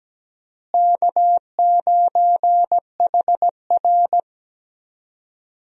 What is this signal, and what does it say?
Morse code, heard as a keyed tone, about 17 wpm, 700 Hz. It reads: K9HR
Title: USA Call Signs - 4 Characters - Repeated 3x - 17wpm